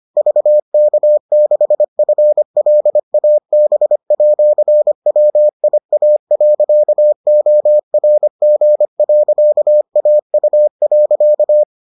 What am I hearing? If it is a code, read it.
Morse code, heard as a keyed tone, about 25 wpm, 600 Hz. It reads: VK6FLAB@WIA.ORG.AU.